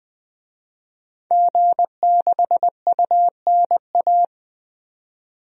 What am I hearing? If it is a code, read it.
Morse code, heard as a keyed tone, about 20 wpm, 700 Hz. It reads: G6UNA
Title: DXCC-100 Call Signs - 5 Characters - 20wpm